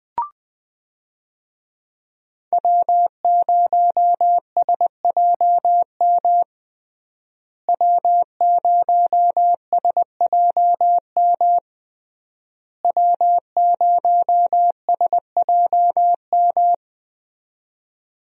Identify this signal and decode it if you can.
Morse code, heard as a keyed tone, about 20 wpm, 700 Hz. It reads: W0SJM W0SJM W0SJM
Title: USA Call Signs - 5 Characters - Repeated 3x - 20wpm